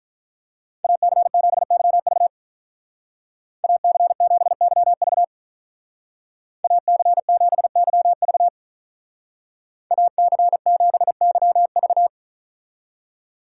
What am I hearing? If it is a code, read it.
Morse code, heard as a keyed tone, about 44 wpm, 700 Hz. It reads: AC7YV AC7YV AC7YV AC7YV
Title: Speed Racing - USA 2x2 Call Signs 35wpm